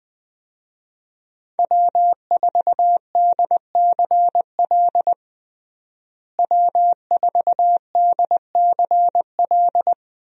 Morse code, heard as a keyed tone, about 20 wpm, 700 Hz. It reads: W4DCL W4DCL
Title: USA Call Signs - 5 Characters - Repeated 2x - 20wpm